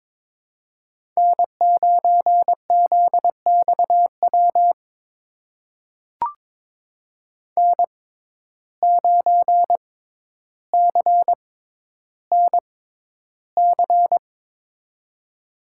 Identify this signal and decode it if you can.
Morse code, heard as a keyed tone, about 22 wpm, spaced as for 11 wpm, 700 Hz. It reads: N9ZXW N9CNC
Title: USA Call Signs - 5 Characters - 6x Character Spacing - 22wpm